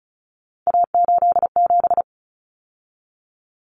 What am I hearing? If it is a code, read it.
Morse code, heard as a keyed tone, about 34 wpm, 700 Hz. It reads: A87
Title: Common Cut Numbers - 3-Digit 35wpm